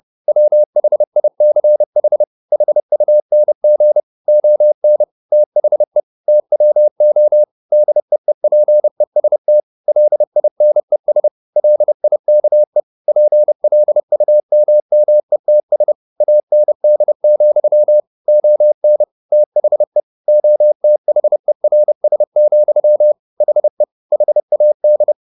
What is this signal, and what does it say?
Morse code, heard as a keyed tone, about 30 wpm, 600 Hz. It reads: WHICH HUNG ON THE TWO DEEPEST LINES LIKE PLUMMETS AND, ON THE OTHERS, HE HAD
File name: the old man and the sea / the old man and the sea 1 30wpm